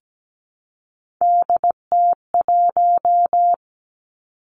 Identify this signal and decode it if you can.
Morse code, heard as a keyed tone, about 17 wpm, 700 Hz. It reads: DT1
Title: Three Challenging Letter-Number Combinations 17wpm